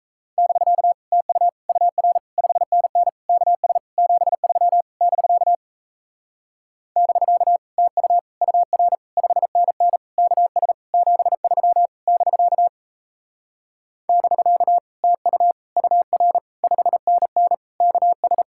Morse code, heard as a keyed tone, about 37 wpm, 700 Hz. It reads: <BK> TU UR 5NN KS 73 <BK> <BK> TU UR 5NN KS 73 <BK> <BK> TU UR 5NN KS